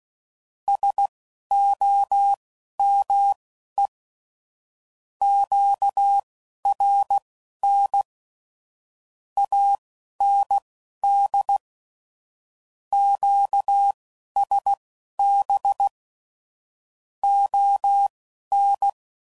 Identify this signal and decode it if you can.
Morse code, heard as a keyed tone, about 16 wpm, spaced as for 11 wpm, 800 Hz. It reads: SOME QRN AND QSB ON
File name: CW Code Groups @15WPM / QSOs / WD0BGP Spaced